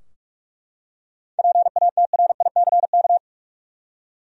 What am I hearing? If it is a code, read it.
Morse code, heard as a keyed tone, about 43 wpm, 700 Hz. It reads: PATRICK
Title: USA Common Names - 45wpm